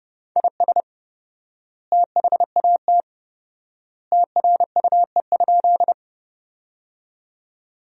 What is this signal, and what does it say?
Morse code, heard as a keyed tone, about 30 wpm, 700 Hz. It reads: IS THAT TRUE?
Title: 4x Word-Spacing - Sentences from Top 400 Words - 30wpm